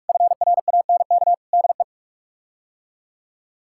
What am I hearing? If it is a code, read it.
Morse code, heard as a keyed tone, about 41 wpm, 700 Hz. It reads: FRANK DE
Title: K1USN SST QSO Practice - 45wpm